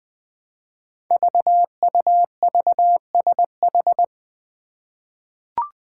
Morse code, heard as a keyed tone, about 20 wpm, 700 Hz. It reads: VUVSH
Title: Five Challenging Letter-Number Combinations 20wpm